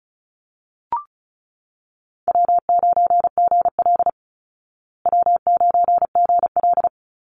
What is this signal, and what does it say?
Morse code, heard as a keyed tone, about 34 wpm, 700 Hz. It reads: W9GL W9GL
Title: Call Signs Repeated 2x - 35wpm